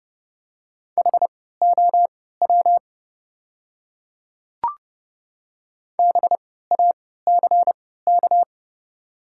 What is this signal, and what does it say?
Morse code, heard as a keyed tone, about 30 wpm, spaced as for 17 wpm, 700 Hz. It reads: HOW BACK
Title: 3x Character Spacing - Top 300 Words - 30wpm